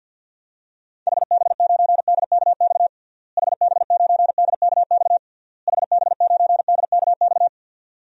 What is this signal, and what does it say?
Morse code, heard as a keyed tone, about 47 wpm, 700 Hz. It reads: HB9DKX HB9DKX HB9DKX
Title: DXCC-100 Call Signs - 6 Characters - Repeated 3x - 50wpm